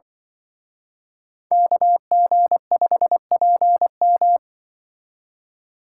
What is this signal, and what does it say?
Morse code, heard as a keyed tone, about 24 wpm, 700 Hz. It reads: KG5PM
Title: Speed Racing - USA 2x2 Call Signs 20wpm